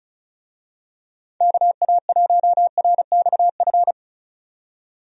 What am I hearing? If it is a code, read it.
Morse code, heard as a keyed tone, about 34 wpm, 700 Hz. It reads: KA1RXF
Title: USA Call Signs - 6 Characters - 35wpm